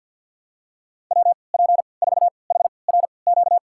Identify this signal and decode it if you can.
Morse code, heard as a keyed tone, about 50 wpm, spaced as for 28 wpm, 700 Hz. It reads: WP4HRQ
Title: USA Call Signs - 6 Characters - 3x Character Spacing - 50wpm